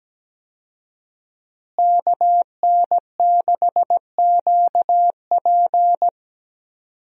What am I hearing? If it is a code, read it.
Morse code, heard as a keyed tone, about 17 wpm, 700 Hz. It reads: KN6QP
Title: USA Call Signs - 5 Characters - Repeated 3x - 17wpm